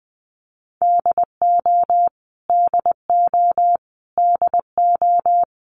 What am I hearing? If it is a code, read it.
Morse code, heard as a keyed tone, about 20 wpm, 700 Hz. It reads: DO DO DO
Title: Top 200 Words Repeated 20wpm